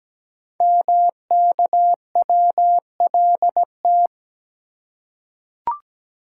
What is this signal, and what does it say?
Morse code, heard as a keyed tone, about 17 wpm, 700 Hz. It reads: MKWLT
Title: Code Groups - 5 Letter:Numbers - 17wpm